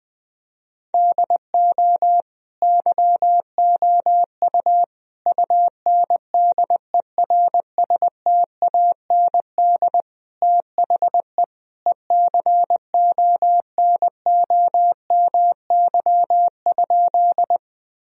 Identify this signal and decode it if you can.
Morse code, heard as a keyed tone, about 20 wpm, 700 Hz. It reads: DO YOU UNDERSTAND THE ECONOMY?
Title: Sentences from Top 1000 Words - Review Part 3 - 20wpm